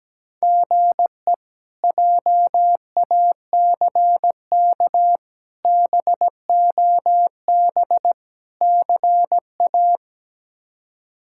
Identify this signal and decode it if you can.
Morse code, heard as a keyed tone, about 17 wpm, 700 Hz. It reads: GE JACK BOB CA
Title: K1USN SST QSO Practice - 17wpm